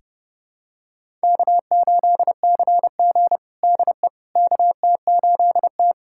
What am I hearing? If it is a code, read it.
Morse code, heard as a keyed tone, about 32 wpm, 700 Hz. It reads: K8CG DE KT8T
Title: USA Callsign DE Callsign - 4 Characters - 30wpm